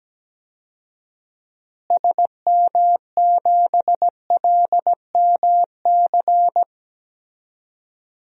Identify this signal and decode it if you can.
Morse code, heard as a keyed tone, about 17 wpm, 700 Hz. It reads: SM7LMC
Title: DXCC-100 Call Signs - 6 Characters - Repeated 2x - 17wpm